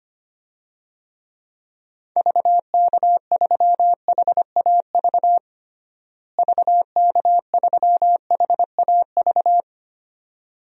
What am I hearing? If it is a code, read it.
Morse code, heard as a keyed tone, about 25 wpm, 700 Hz. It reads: VK3HAV VK3HAV
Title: DXCC-100 Call Signs - 6 Characters - Repeated 2x - 25wpm